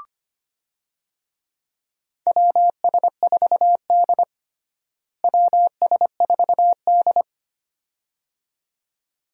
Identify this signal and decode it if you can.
Morse code, heard as a keyed tone, about 25 wpm, 700 Hz. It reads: WS4D WS4D
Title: USA Call Signs - 4 Characters - Repeated 2x - 25wpm